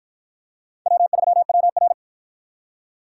Morse code, heard as a keyed tone, about 48 wpm, 700 Hz. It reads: W3WR